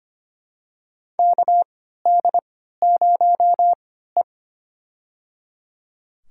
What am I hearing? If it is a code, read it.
Morse code, heard as a keyed tone, about 25 wpm, spaced as for 14 wpm, 700 Hz. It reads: KD0E
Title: USA Call Signs - 3x Character Spacing - 25wpm